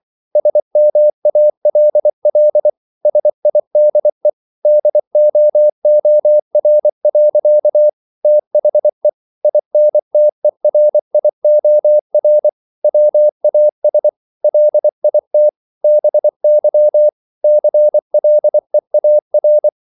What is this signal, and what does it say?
Morse code, heard as a keyed tone, about 24 wpm, 600 Hz. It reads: SMALL SIDE DOOR. THE INTERIOR WAS LIT BY CLEAR